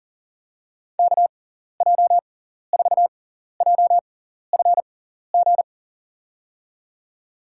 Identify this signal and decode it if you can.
Morse code, heard as a keyed tone, about 38 wpm, spaced as for 14 wpm, 700 Hz. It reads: KJ4JFG
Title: USA Call Signs - 6x Character Spacing - 40wpm